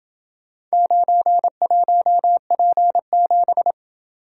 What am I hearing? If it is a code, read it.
Morse code, heard as a keyed tone, about 28 wpm, 700 Hz. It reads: 91P7